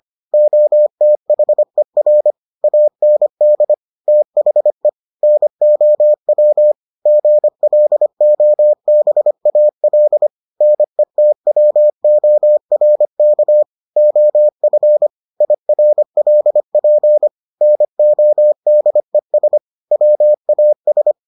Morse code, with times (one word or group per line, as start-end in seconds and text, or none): 0.3-2.3 OTHER
2.6-3.7 AND
4.1-4.9 THE
5.2-6.7 NOW
7.1-10.3 GLOBAL
10.6-13.6 NETWORK
14.0-15.1 OF
15.4-17.3 IRLP
17.6-19.6 NODES
19.9-21.1 WAS